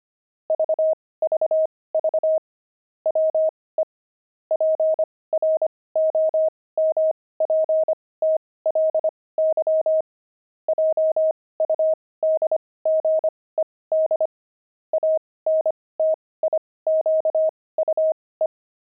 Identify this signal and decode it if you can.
Morse code, heard as a keyed tone, about 25 wpm, spaced as for 18 wpm, 630 Hz. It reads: VVV WE PROMPTLY JUDGED ANTIQUE
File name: pangram-fs-18-cs-25-buckles